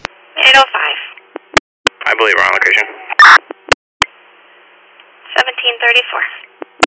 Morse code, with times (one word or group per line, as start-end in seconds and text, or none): none